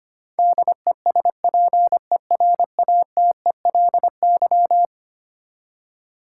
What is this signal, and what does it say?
Morse code, heard as a keyed tone, about 25 wpm, 700 Hz. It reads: DESPERATELY
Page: morse-code-ninja-practice-25wpm